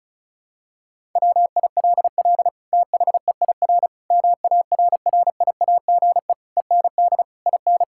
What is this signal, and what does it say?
Morse code, heard as a keyed tone, about 34 wpm, 700 Hz. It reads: WILL THEIR MARRIAGE END IN
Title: Sentences from Top 1000 Words - Review Part 6 - 35wpm